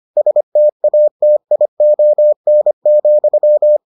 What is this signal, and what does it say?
Morse code, heard as a keyed tone, about 25 wpm, 600 Hz. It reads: STATION,